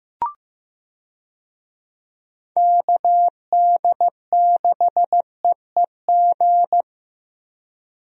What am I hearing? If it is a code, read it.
Morse code, heard as a keyed tone, about 16 wpm, 700 Hz. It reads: KD6EEG
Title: USA Call Signs - 6 Characters - Repeated 2x - 15wpm